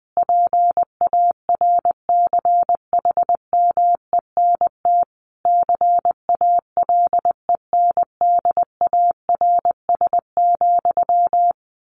Morse code, heard as a keyed tone, about 20 wpm, 700 Hz. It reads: PARCHMENT CALENDARS,
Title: A Sound Of Thunder 20wpm